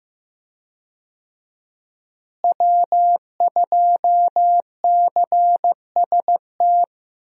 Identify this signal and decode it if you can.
Morse code, heard as a keyed tone, about 15 wpm, 700 Hz. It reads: W2CST
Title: USA Call Signs - 5 Characters - Repeated 2x - 15wpm